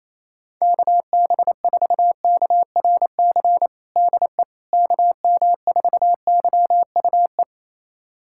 Morse code, heard as a keyed tone, about 28 wpm, 700 Hz. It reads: KB4KRC DE KM4YUE
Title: USA Callsign DE Callsign - 6 Characters - 28wpm